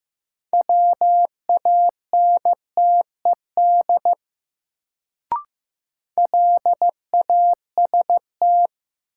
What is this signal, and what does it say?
Morse code, heard as a keyed tone, about 15 wpm, 700 Hz. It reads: WANTED LAST
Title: Sets of 2 Words - New Word Drill - 15wpm